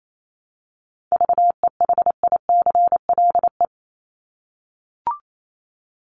Morse code, heard as a keyed tone, about 28 wpm, 700 Hz. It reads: VEHICLE
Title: Top 1751-2000 Words 28wpm